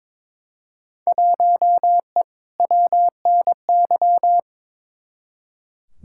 Morse code, heard as a keyed tone, about 22 wpm, 700 Hz. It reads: 1E WNY
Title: Field Day Exchanges - 22wpm